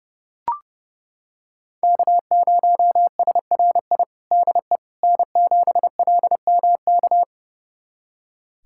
Morse code, heard as a keyed tone, about 30 wpm, 700 Hz. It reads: K0SRI DE N7LMK
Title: USA Callsign DE Callsign - 5 Characters - 30wpm